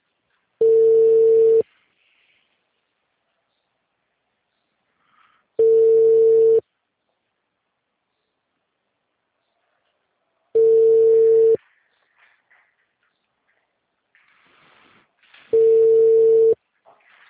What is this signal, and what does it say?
Ringback tone (China, 450 Hz).